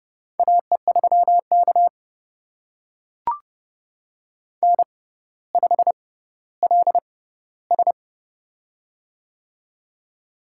Morse code, heard as a keyed tone, about 28 wpm, spaced as for 15 wpm, 700 Hz. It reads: AE3K N5LS
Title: USA Call Signs - 4 Characters - 6x Character Spacing - 30wpm